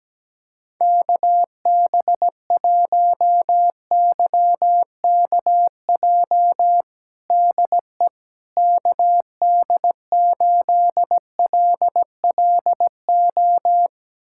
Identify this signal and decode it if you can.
Morse code, heard as a keyed tone, about 17 wpm, 700 Hz. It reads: KB1YKJ DE KD8LLO
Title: USA Callsign DE Callsign - 6 Characters - 17wpm